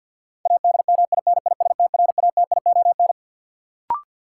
Morse code, heard as a keyed tone, about 53 wpm, 700 Hz. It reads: ADMINISTRATION
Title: Sentences from Top 1300 Words - New Word Drill Alt - 50wpm